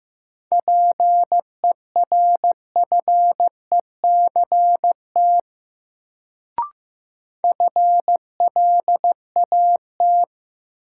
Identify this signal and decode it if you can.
Morse code, heard as a keyed tone, about 16 wpm, 700 Hz. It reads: PERFECT FLAT
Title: Sentences from Top 1000 Words - New Word Drill - 15wpm